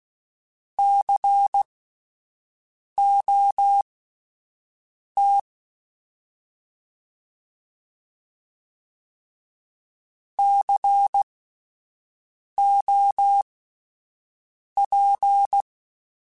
Morse code, heard as a keyed tone, about 16 wpm, spaced as for 5 wpm, 800 Hz. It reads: COT COP
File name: Spaced Alphas / Spaced Alphas mp3s / Cs sp @10